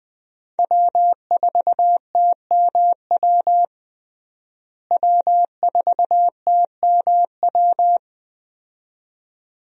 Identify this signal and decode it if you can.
Morse code, heard as a keyed tone, about 20 wpm, 700 Hz. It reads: W4TMW W4TMW
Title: USA Call Signs - 5 Characters - Repeated 2x - 20wpm